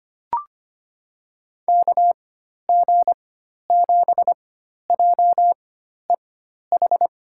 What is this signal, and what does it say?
Morse code, heard as a keyed tone, about 25 wpm, spaced as for 12 wpm, 700 Hz. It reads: KG7JEH
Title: USA Call Signs - 6 Characters - 4x Character Spacing - 25wpm